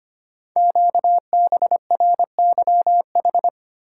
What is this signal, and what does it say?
Morse code, heard as a keyed tone, about 25 wpm, 700 Hz. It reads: QBRYH
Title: Code Groups - 5 Letters - 25wpm